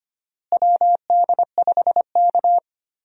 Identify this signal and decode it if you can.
Morse code, heard as a keyed tone, about 25 wpm, 700 Hz. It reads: WD5K
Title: USA Call Signs - 4 Characters - Repeated 3x - 25wpm